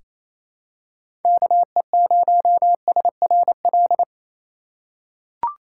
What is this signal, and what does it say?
Morse code, heard as a keyed tone, about 28 wpm, 700 Hz. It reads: KE0SRL
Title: K1USN SST QSO Practice - 28wpm